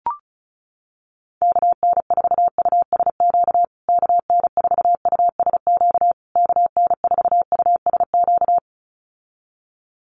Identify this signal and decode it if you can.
Morse code, heard as a keyed tone, about 34 wpm, 700 Hz. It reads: KN4USQ KN4USQ KN4USQ
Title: USA Call Signs Repeated 3x 35wpm